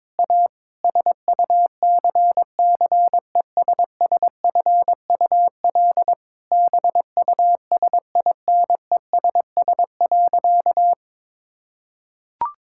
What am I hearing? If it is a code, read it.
Morse code, heard as a keyed tone, about 22 wpm, 700 Hz. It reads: A SUCCESSFUL BUSINESS.
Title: Sentences from Top 1000 Words - Review Part 6 - 22wpm